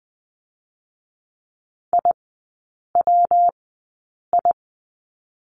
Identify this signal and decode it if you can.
Morse code, heard as a keyed tone, about 20 wpm, 700 Hz. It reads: I W I
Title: 114 Lesson 19e - Letter w - Three Characters 20wpm